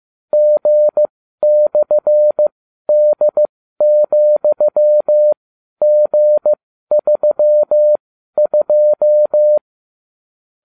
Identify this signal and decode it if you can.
Morse code, heard as a keyed tone, about 15 wpm, spaced as for 12 wpm, 600 Hz. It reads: G/D,G32